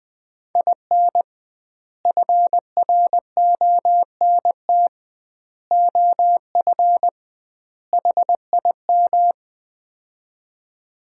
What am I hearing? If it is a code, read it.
Morse code, heard as a keyed tone, about 20 wpm, 700 Hz. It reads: IN FRONT OF HIM
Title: 2x Word-Spacing - Sets of 4 Words - 20wpm